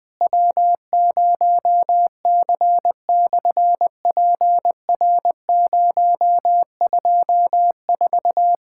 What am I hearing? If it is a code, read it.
Morse code, heard as a keyed tone, about 20 wpm, 700 Hz. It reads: W0C/PR024
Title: SOTA QSO Practice 20wpm